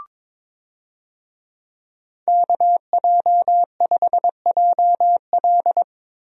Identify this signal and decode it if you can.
Morse code, heard as a keyed tone, about 22 wpm, 700 Hz. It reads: KJ5JL
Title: USA Call Signs - 5 Characters - Repeated 3x - 22wpm